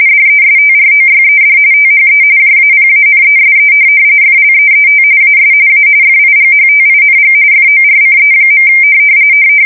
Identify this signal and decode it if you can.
RTTY, about 45.45 baud. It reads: DESTROYED THE JAPANESE FLEET AS A FIGHTING FORCE.
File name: fsk170 / USS-New-Jersey